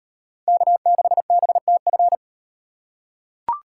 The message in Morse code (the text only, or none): K6BTF